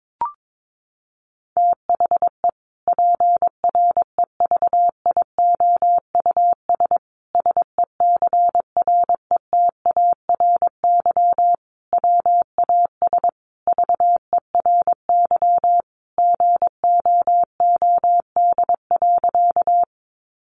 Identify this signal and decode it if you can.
Morse code, heard as a keyed tone, about 22 wpm, 700 Hz. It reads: THE PREVIOUS SECRETARY WAS VERY GOOD.
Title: Sentences from Top 1500 Words 22wpm